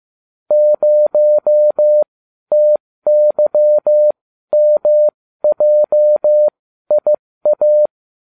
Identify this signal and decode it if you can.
Morse code, heard as a keyed tone, about 15 wpm, spaced as for 12 wpm, 600 Hz. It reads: 0TYMJIA